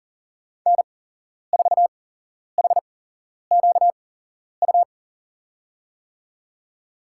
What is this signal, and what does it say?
Morse code, heard as a keyed tone, about 40 wpm, spaced as for 11 wpm, 700 Hz. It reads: N4HQU
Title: USA Call Signs - 5 Characters - 8x Character Spacing - 40wpm